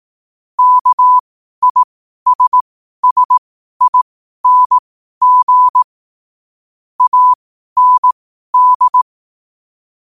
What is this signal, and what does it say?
Morse code, heard as a keyed tone, about 18 wpm, spaced as for 12 wpm, 1000 Hz. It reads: KISSING AND
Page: 12-wpm-morse-code-quotes